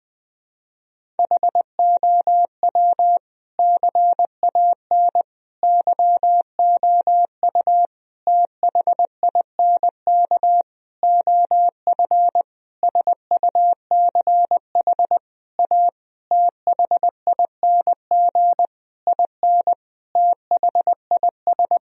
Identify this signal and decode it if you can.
Morse code, heard as a keyed tone, about 21 wpm, 700 Hz. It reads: HOW CAN YOU THINK OF SUCH A THING IN THIS
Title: Sentences from Top 1000 Words - Review Part 3 - 20wpm